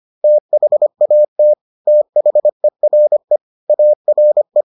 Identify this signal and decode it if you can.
Morse code, heard as a keyed tone, about 25 wpm, 600 Hz. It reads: THAT THERE ARE